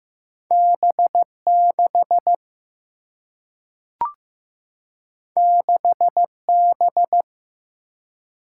Morse code, heard as a keyed tone, about 15 wpm, 700 Hz. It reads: B 6 6 B
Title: Single-Letter Single-Number 15wpm